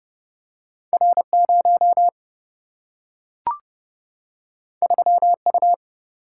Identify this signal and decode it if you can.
Morse code, heard as a keyed tone, about 30 wpm, 700 Hz. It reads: R 0 3 U
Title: Single-Letter Single-Number 30wpm